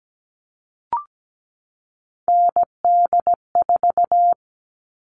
Morse code, heard as a keyed tone, about 17 wpm, 700 Hz. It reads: ND4
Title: All Cut Numbers - 3-Digit 17wpm